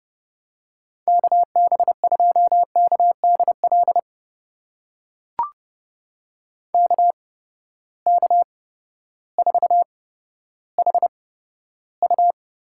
Morse code, heard as a keyed tone, about 30 wpm, spaced as for 14 wpm, 700 Hz. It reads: KB2KDL KK4HU
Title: USA Call Signs - 6 Characters - 8x Character Spacing - 30wpm